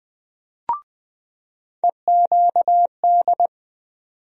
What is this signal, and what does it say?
Morse code, heard as a keyed tone, about 20 wpm, 700 Hz. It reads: EQD
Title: USA Call Sign Postfixes - 3 Letters 20wpm